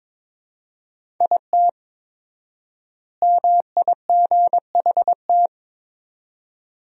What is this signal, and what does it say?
Morse code, heard as a keyed tone, about 21 wpm, 700 Hz. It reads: IT MIGHT
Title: 4x Word-Spacing - Sentences from Top 500 Words - 22wpm